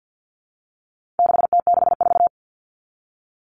Morse code, heard as a keyed tone, about 50 wpm, 700 Hz. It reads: 6T64
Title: All Cut Numbers - 4-Digit 50wpm